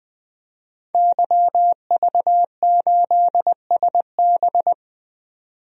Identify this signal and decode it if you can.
Morse code, heard as a keyed tone, about 21 wpm, 700 Hz. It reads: YV8SB
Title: DXCC-100 Call Signs - 5 Characters - 20wpm